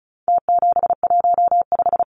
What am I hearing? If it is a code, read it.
Morse code, heard as a keyed tone, about 34 wpm, 700 Hz. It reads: T715
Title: Numbers - Contest Style - 4-Digit 35wpm